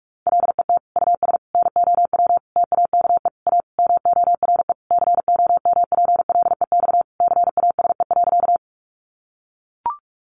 Morse code, heard as a keyed tone, about 44 wpm, 700 Hz. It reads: LET US NOW TAKE A MORE COMPLEX CASE.
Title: Sentences from Top 900 Words 45wpm